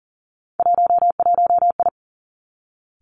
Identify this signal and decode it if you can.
Morse code, heard as a keyed tone, about 41 wpm, 700 Hz. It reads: 11I
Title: Three Challenging Letter-Number Combinations 40wpm